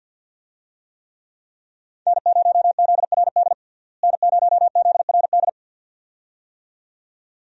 Morse code, heard as a keyed tone, about 53 wpm, 700 Hz. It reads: N0ZRD N0ZRD
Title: USA Call Signs - 5 Characters - Repeated 2x - 50wpm